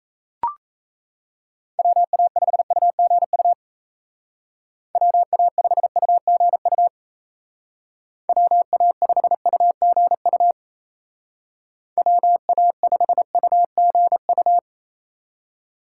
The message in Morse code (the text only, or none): WA5UGU WA5UGU WA5UGU WA5UGU